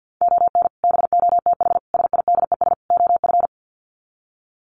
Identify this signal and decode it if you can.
Morse code, heard as a keyed tone, about 50 wpm, 700 Hz. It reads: ON BOTH SIDES OF
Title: Sets of 4 Words 50wpm